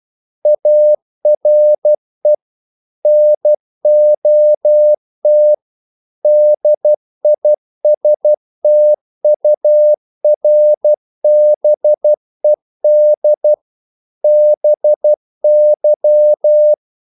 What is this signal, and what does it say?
Morse code, heard as a keyed tone, about 12 wpm, 600 Hz. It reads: ARE NOT DISTURBED BY
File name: CW / Practice / EnglishQuotations0312